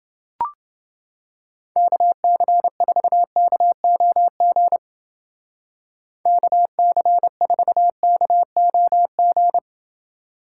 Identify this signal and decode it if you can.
Morse code, heard as a keyed tone, about 28 wpm, 700 Hz. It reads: KC4KOG KC4KOG